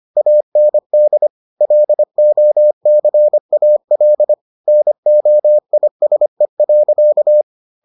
Morse code, heard as a keyed tone, about 25 wpm, 600 Hz. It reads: AND LOCAL NOISE.